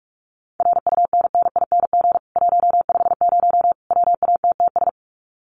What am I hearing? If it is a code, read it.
Morse code, heard as a keyed tone, about 44 wpm, 700 Hz. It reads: RUNNING 150 WATTS
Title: Common QSO Phrases 45wpm